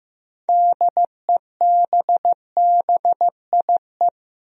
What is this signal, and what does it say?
Morse code, heard as a keyed tone, about 15 wpm, 700 Hz. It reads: DEBBIE
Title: USA Common Names - 15wpm